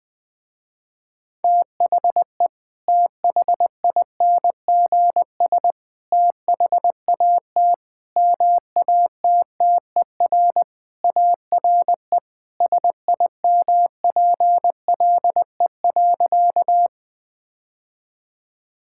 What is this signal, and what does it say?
Morse code, heard as a keyed tone, about 20 wpm, 700 Hz. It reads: THE THINGS THAT MATTER ARE SIMPLE.
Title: Sentences with 3-Word Phrases - Spoken Beginning - 20wpm